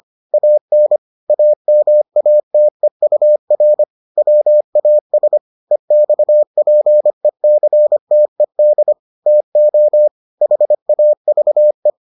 Morse code, heard as a keyed tone, about 25 wpm, 600 Hz. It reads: AN AMATEUR WAS EXPECTED TO HAVE